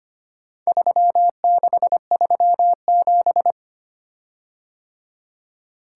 Morse code, heard as a keyed tone, about 25 wpm, 700 Hz. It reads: 3637
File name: practice / 25wpm / Four Letter-Number Combinations - 25wpm